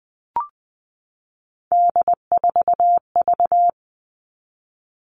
Morse code, heard as a keyed tone, about 20 wpm, 700 Hz. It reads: D4V
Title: All Cut Numbers - 3-Digit 20wpm